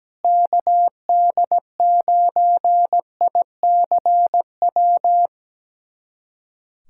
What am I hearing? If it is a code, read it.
Morse code, heard as a keyed tone, about 17 wpm, 700 Hz. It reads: KD9ICW
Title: USA Callsign DE Callsign 17wpm